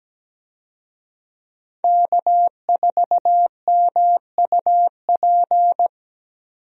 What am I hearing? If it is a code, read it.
Morse code, heard as a keyed tone, about 17 wpm, 700 Hz. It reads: K4MUP
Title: USA Call Signs - 5 Characters - 17wpm